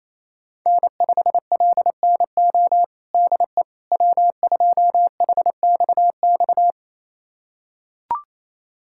Morse code, heard as a keyed tone, about 27 wpm, 700 Hz. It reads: N5LNO DE W2HXX
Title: USA Callsign DE Callsign - 5 Characters - 28wpm